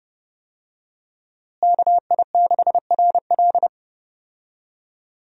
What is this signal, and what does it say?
Morse code, heard as a keyed tone, about 30 wpm, 700 Hz. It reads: KI6RL
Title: USA Call Signs - 5 Characters - Repeated 2x - 30wpm